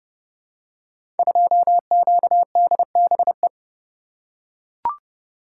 Morse code, heard as a keyed tone, about 28 wpm, 690 Hz. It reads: 2QDBE
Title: Code Groups - 5 Letter:Numbers - 30wpm